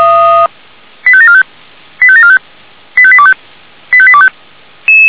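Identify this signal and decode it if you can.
Five-tone selective call (ZVEI), heard as quick runs of five tones, tones of about 70 ms. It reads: 85745 85745 85825 85825
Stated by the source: ZVEI five-tone call received by SDR.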